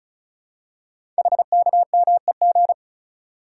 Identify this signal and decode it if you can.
Morse code, heard as a keyed tone, about 36 wpm, 700 Hz. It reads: HKMEG
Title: Code Groups - 5 Letters - 35wpm